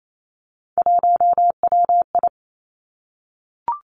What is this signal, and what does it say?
Morse code, heard as a keyed tone, about 28 wpm, 700 Hz. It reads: 1WI
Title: Three Challenging Letter-Number Combinations 28wpm